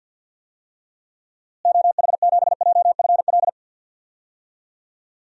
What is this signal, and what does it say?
Morse code, heard as a keyed tone, about 49 wpm, 700 Hz. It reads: OH7JFL
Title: DXCC-100 Call Signs - 6 Characters - Repeated 3x - 50wpm